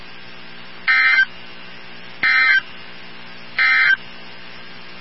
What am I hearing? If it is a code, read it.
Emergency Alert System (SAME) tones, heard as harsh buzzing bursts. It is heard from 0.8 to 4.0 s.
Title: NOAA Weather Radio 162.550